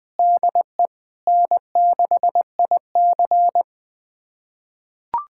Morse code, heard as a keyed tone, about 20 wpm, 700 Hz. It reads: DE N6IC
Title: USA Callsign DE Callsign - 4 Characters - 20wpm